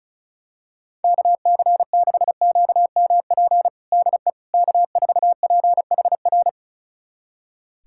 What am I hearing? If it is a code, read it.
Morse code, heard as a keyed tone, about 36 wpm, 700 Hz. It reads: KC6QMP DE K4PHR
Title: USA Callsign DE Callsign 35wpm